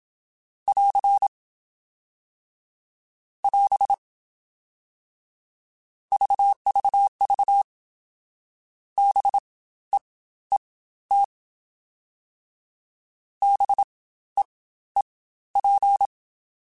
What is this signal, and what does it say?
Morse code, heard as a keyed tone, about 27 wpm, spaced as for 14 wpm, 800 Hz. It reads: <AR> <AS> VVV BEET BEEP
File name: Spaced Alphas / Spaced Alphas mp3s / Bs sp @25